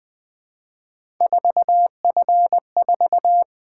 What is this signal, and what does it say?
Morse code, heard as a keyed tone, about 20 wpm, 700 Hz. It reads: 4F4
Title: Three Letter-Number Combinations - 20wpm